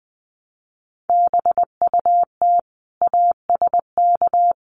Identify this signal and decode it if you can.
Morse code, heard as a keyed tone, about 20 wpm, 700 Hz. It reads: BUT ASK
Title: Sentences from Top 600 Words 20wpm